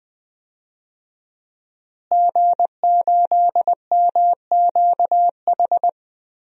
Morse code, heard as a keyed tone, about 20 wpm, 700 Hz. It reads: G8MQH
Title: DXCC-100 Call Signs - 5 Characters - Repeated 3x - 20wpm